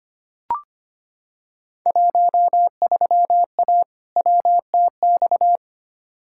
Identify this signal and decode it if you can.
Morse code, heard as a keyed tone, about 25 wpm, 700 Hz. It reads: 13A WTX
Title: Field Day Exchanges Encore - 25wpm